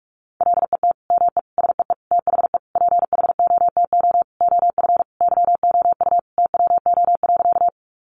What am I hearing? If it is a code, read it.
Morse code, heard as a keyed tone, about 44 wpm, 700 Hz. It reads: LET ME SEE THE PHOTO OF YOU TWO.
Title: Sentences from Top 1200 Words 45wpm